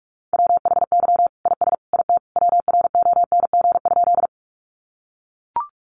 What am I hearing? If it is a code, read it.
Morse code, heard as a keyed tone, about 44 wpm, 700 Hz. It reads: WHY IS IT WRONG?
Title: Sentences from Top 900 Words 45wpm